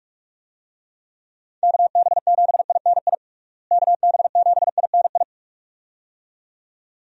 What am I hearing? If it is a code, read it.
Morse code, heard as a keyed tone, about 46 wpm, 700 Hz. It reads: KB7INI KB7INI
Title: USA Call Signs - 6 Characters - Repeated 2x - 45wpm